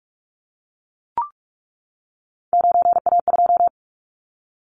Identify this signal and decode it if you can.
Morse code, heard as a keyed tone, about 44 wpm, 700 Hz. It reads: 9A2
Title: Callsign Prefixes for Top 25 DXCC - 45wpm